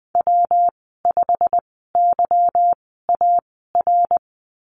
Morse code, heard as a keyed tone, about 20 wpm, spaced as for 15 wpm, 700 Hz. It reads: W5YAR
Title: 186 Lesson 30g - Letter b - Call Signs 20wpm